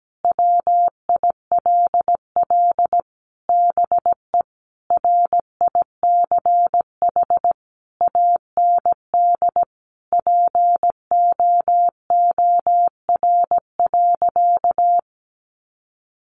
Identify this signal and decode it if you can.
Morse code, heard as a keyed tone, about 17 wpm, 700 Hz. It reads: WILL BE RICH AND POOR.
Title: Sentences from Top 1100 Words 17wpm